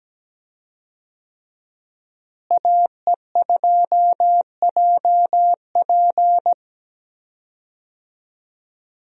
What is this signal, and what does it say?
Morse code, heard as a keyed tone, about 17 wpm, 700 Hz. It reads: AE2JP
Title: USA Call Signs - 5 Characters - Repeated 3x - 17wpm